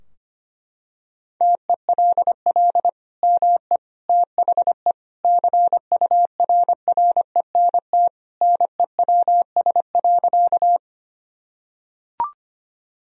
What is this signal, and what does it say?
Morse code, heard as a keyed tone, about 25 wpm, 700 Hz. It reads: TELL ME THE CURRENT NEWS.
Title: Sentences from Top 1000 Words - Review Part 4 - 25wpm